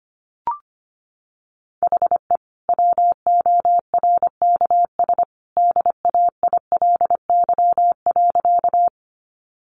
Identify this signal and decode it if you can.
Morse code, heard as a keyed tone, about 25 wpm, 700 Hz. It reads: HE WORKS DAILY.
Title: Sentences from Top 1100 Words 25wpm